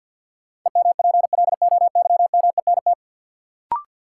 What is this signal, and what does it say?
Morse code, heard as a keyed tone, about 53 wpm, 700 Hz. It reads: EMPLOYMENT
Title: Sentences from Top 1500 Words - New Word Drill Alt - 50wpm